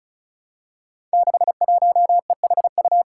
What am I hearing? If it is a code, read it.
Morse code, heard as a keyed tone, about 34 wpm, 700 Hz. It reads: 61EHU
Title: Five Challenging Letter-Number Combinations 35wpm